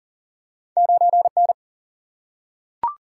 Morse code, heard as a keyed tone, about 42 wpm, 700 Hz. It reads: 9N
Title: Single-Letter Single-Number 40wpm